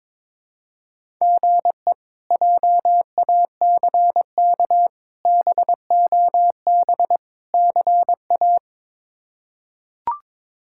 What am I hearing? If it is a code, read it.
Morse code, heard as a keyed tone, about 22 wpm, 700 Hz. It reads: GE JACK BOB CA
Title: K1USN SST QSO Practice - 22wpm